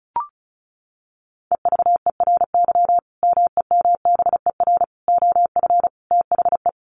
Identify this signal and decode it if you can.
Morse code, heard as a keyed tone, about 34 wpm, 700 Hz. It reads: EVERY MEMBER OF THE
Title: Sentences from Top 1400 Words 35wpm